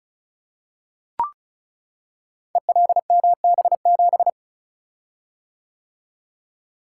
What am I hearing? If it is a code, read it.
Morse code, heard as a keyed tone, about 35 wpm, 700 Hz. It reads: ELMB7
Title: Code Groups - 5 Letter:Numbers - 35wpm